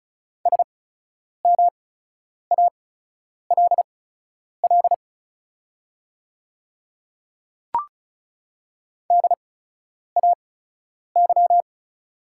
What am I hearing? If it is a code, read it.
Morse code, heard as a keyed tone, about 36 wpm, spaced as for 10 wpm, 700 Hz. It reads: SMALL DAY
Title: 8x Character Spacing - Top 200 Words - 35wpm